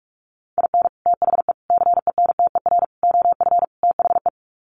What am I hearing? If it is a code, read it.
Morse code, heard as a keyed tone, about 44 wpm, 700 Hz. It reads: IN THE CENTER OF THE
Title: Sets of 5 Words 45wpm